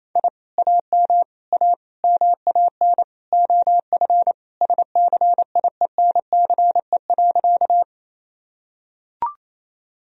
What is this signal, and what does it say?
Morse code, heard as a keyed tone, about 28 wpm, 700 Hz. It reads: I AM A MAN OF SCIENCE.
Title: Sentences from Top 1000 Words - Review Part 2 - 28wpm